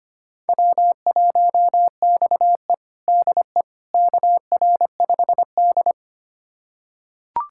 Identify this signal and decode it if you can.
Morse code, heard as a keyed tone, about 25 wpm, 700 Hz. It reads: W1XE DE KR5D
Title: USA Callsign DE Callsign - 4 Characters - 25wpm